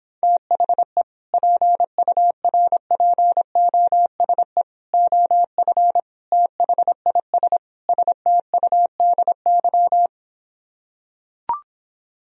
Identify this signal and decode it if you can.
Morse code, heard as a keyed tone, about 26 wpm, 700 Hz. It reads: THE PURPOSE OF THIS STUDY